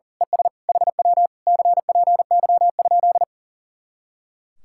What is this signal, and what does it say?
Morse code, heard as a keyed tone, about 37 wpm, 700 Hz. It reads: ES HW CPY?
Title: Speed-Racing Common QSO Phrases - 40wpm